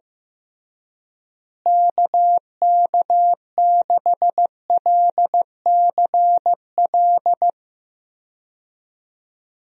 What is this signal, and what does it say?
Morse code, heard as a keyed tone, about 15 wpm, 700 Hz. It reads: KK6LCL
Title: USA Call Signs - 6 Characters - Repeated 2x - 15wpm